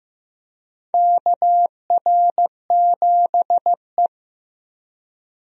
Morse code, heard as a keyed tone, about 15 wpm, 700 Hz. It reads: KR7E
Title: USA Call Signs - 4 Characters - 15wpm